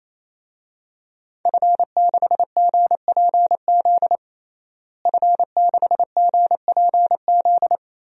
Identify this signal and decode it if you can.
Morse code, heard as a keyed tone, about 28 wpm, 700 Hz. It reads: F6GPZ F6GPZ
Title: DXCC-100 Call Signs - 5 Characters - Repeated 3x - 28wpm